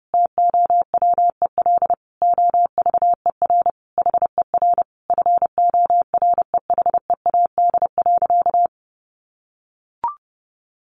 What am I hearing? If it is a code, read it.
Morse code, heard as a keyed tone, about 30 wpm, 700 Hz. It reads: TOWEL OVER HER FOREHEAD.
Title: Easy Phrases 30wpm